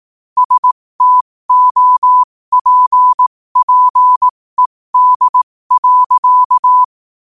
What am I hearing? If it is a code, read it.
Morse code, heard as a keyed tone, about 18 wpm, spaced as for 15 wpm, 1000 Hz. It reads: STOPPED.
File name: qotdfiles / 15wpm